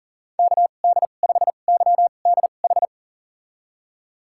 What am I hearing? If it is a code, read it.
Morse code, heard as a keyed tone, about 40 wpm, spaced as for 29 wpm, 700 Hz. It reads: KD5YDH
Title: USA Call Signs - 6 Characters - 2x Character Spacing - 40wpm